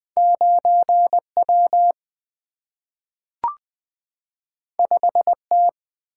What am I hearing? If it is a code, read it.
Morse code, heard as a keyed tone, about 20 wpm, 700 Hz. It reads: 9 W 5 T
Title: Single-Letter Single-Number 20wpm